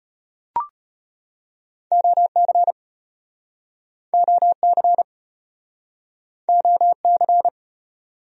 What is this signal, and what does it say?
Morse code, heard as a keyed tone, about 36 wpm, 700 Hz. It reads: OC OC OC